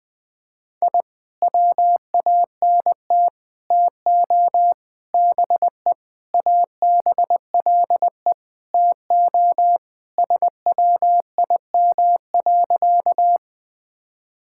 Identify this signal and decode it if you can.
Morse code, heard as a keyed tone, about 20 wpm, 700 Hz. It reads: I WANT TO BE ABLE TO SWIM.
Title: Sentences with 5-Word Phrases - 20wpm